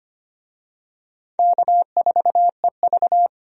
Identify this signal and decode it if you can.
Morse code, heard as a keyed tone, about 25 wpm, 700 Hz. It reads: K4EV
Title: USA Call Signs - 4 Characters - 25wpm